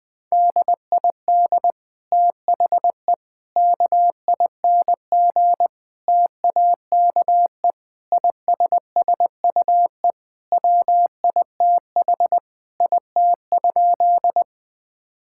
Morse code, heard as a keyed tone, about 20 wpm, 700 Hz. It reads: DID THE KING TAKE ISSUE WITH IT?
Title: Sentences from Top 1000 Words - Review Part 6 - 20wpm